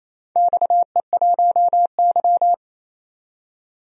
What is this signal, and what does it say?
Morse code, heard as a keyed tone, about 28 wpm, 700 Hz. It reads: XE1Y
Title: DXCC-100 Call Signs - 4 Characters - 28wpm